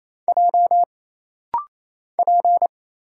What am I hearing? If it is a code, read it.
Morse code, heard as a keyed tone, about 27 wpm, 700 Hz. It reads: J P
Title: Single Letter-Number Alt - Rapid-Fire 28wpm